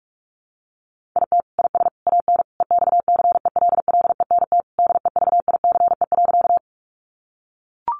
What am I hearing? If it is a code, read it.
Morse code, heard as a keyed tone, about 44 wpm, 700 Hz. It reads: IT IS AN EXCELLENT DEVICE.
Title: Sentences from Top 1500 Words 45wpm